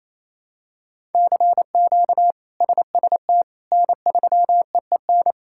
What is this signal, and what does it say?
Morse code, heard as a keyed tone, about 28 wpm, 700 Hz. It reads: CQ SST N3EEN
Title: K1USN SST QSO Practice - 28wpm